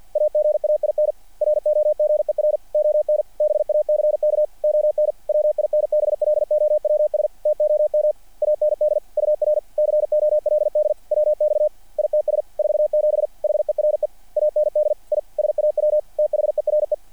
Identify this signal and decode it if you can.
Morse code, heard as a keyed tone, about 48 wpm, 600 Hz. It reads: AGAIN ROGER ON BACK ON WINBLOWS TOM AND UR COLD WX ITS 47 HERE AND I SAW THERE
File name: archived / CRSnet